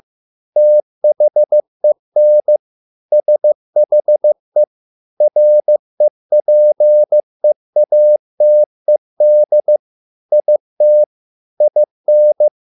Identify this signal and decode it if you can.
Morse code, heard as a keyed tone, about 15 wpm, 600 Hz. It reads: THEN SHE REPEATED IT IN